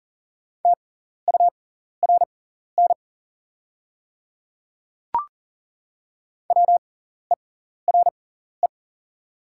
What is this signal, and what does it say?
Morse code, heard as a keyed tone, about 40 wpm, spaced as for 14 wpm, 690 Hz. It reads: TURN WERE
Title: 6x Character Spacing - Top 200 Words - 40wpm